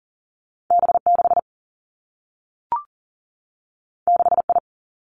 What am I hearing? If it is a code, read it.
Morse code, heard as a keyed tone, about 39 wpm, 700 Hz. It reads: B 6 6 I
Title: Two Challenging Letter-Number Combinations 40wpm